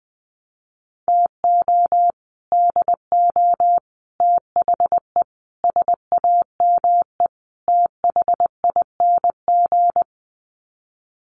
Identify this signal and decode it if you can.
Morse code, heard as a keyed tone, about 20 wpm, 700 Hz. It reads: TO DO THE SAME THING
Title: Sets of 5 Words Encore 20wpm